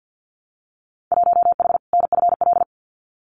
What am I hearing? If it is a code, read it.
Morse code, heard as a keyed tone, about 50 wpm, 700 Hz. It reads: 1H NFL
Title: Winter Field Day Exchanges 50wpm